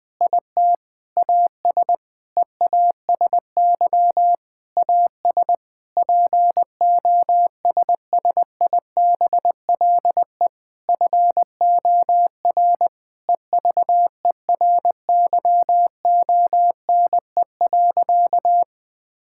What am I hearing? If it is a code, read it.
Morse code, heard as a keyed tone, about 20 wpm, 700 Hz. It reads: IT AS EASY AS POSSIBLE FOR EVERYONE.
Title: Sentences with 8-Word Phrases - Spoken Beginning - 20wpm